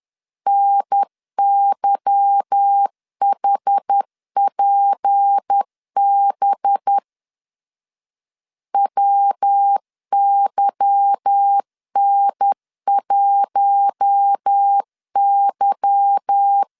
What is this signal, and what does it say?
Morse code, heard as a keyed tone, about 10 wpm, 800 Hz. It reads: NYHPB WYN1Y
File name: LEZIONI / 8 YJ45 / 8 Y 10wpm 2615